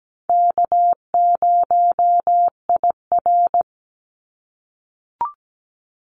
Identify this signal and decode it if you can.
Morse code, heard as a keyed tone, about 17 wpm, 700 Hz. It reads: K0IR
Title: Call Signs - 17wpm